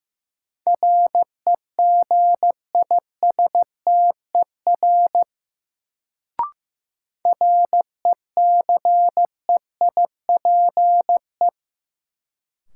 Morse code, heard as a keyed tone, about 16 wpm, 700 Hz. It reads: REGISTER RECEIPE
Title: Sentences from Top 1200 Words - New Word Drill - 15wpm